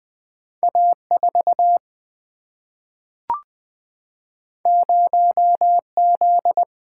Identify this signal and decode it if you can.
Morse code, heard as a keyed tone, about 20 wpm, 700 Hz. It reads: A4 0Z
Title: Single-Letter Single-Number 20wpm